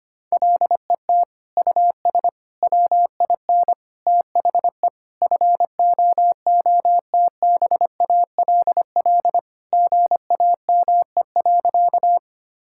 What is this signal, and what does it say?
Morse code, heard as a keyed tone, about 25 wpm, 700 Hz. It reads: LET US WIN THE FOOTBALL GAME.
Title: Sentences from Top 1600 Words 25wpm